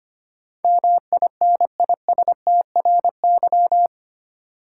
Morse code, heard as a keyed tone, about 25 wpm, 700 Hz. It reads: MINISTRY
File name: practice / 25wpm / 8 Letter Words - First 3 Letters Given - 25wpm